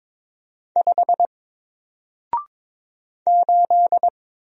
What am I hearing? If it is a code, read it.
Morse code, heard as a keyed tone, about 22 wpm, 700 Hz. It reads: 5 8
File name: Single-Letter-Number-22wpm